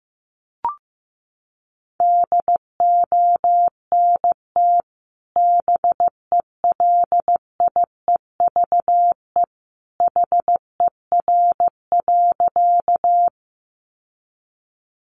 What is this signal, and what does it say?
Morse code, heard as a keyed tone, about 15 wpm, 700 Hz. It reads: DONT BELIEVE HER.
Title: Sentences from Top 1000 Words 15wpm